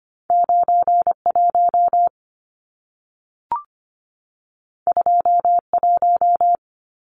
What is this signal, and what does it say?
Morse code, heard as a keyed tone, about 25 wpm, 700 Hz. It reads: 91 21
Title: Numbers - 2-Digit 25wpm